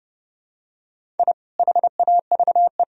Morse code, heard as a keyed tone, about 31 wpm, 700 Hz. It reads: I HAVE
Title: Sentences from Top 1000 Words - Review Part 1 - 30wpm